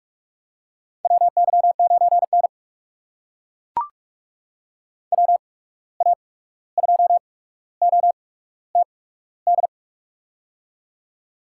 Morse code, heard as a keyed tone, about 42 wpm, spaced as for 17 wpm, 700 Hz. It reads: WY9N WA2OTD
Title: USA Call Signs - 8x Character Spacing - 45wpm